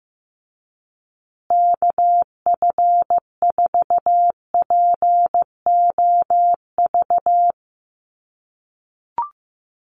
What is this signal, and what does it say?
Morse code, heard as a keyed tone, about 15 wpm, 700 Hz. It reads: KF4POV
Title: USA Call Signs Repeated 3x 15wpm